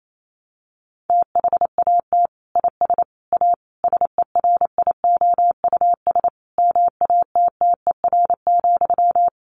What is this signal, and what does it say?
Morse code, heard as a keyed tone, about 28 wpm, 700 Hz. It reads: THAT IS A SERIOUS MATTER,
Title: Sentences from Top 1100 Words 28wpm